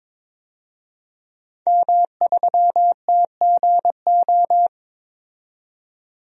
Morse code, heard as a keyed tone, about 22 wpm, 700 Hz. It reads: M3TGO
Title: DXCC-100 Call Signs - 5 Characters - 22wpm